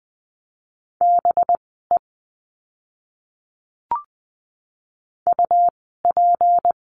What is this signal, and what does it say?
Morse code, heard as a keyed tone, about 20 wpm, spaced as for 15 wpm, 700 Hz. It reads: BE UP
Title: 199 Lesson 32f - Letter k - Words 20wpm